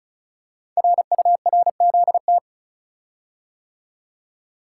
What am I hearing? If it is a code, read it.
Morse code, heard as a keyed tone, about 34 wpm, 700 Hz. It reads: RURZT
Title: Code Groups - 5 Letters - 35wpm